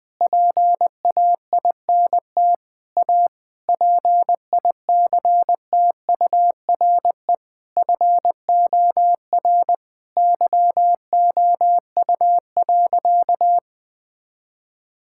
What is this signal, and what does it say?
Morse code, heard as a keyed tone, about 20 wpm, 700 Hz. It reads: PAINT A PICTURE FOR YOU.
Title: Sentences from Top 1000 Words - Review Part 6 - 20wpm